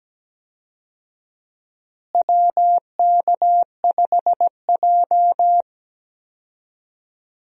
Morse code, heard as a keyed tone, about 17 wpm, 700 Hz. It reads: WK5J
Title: USA Call Signs - 4 Characters - Repeated 3x - 17wpm